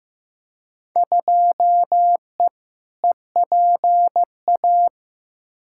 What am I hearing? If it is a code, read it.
Morse code, heard as a keyed tone, about 15 wpm, 700 Hz. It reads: 2E EPA
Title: Field Day Exchanges Encore - 15wpm